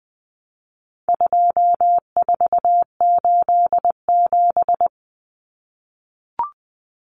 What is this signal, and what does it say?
Morse code, heard as a keyed tone, about 20 wpm, 700 Hz. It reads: 2487
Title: Numbers - 4-Digit 20wpm